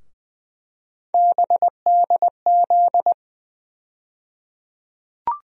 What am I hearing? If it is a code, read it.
Morse code, heard as a keyed tone, about 20 wpm, 700 Hz. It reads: BDZ
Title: USA Call Sign Postfixes - 3 Letters 20wpm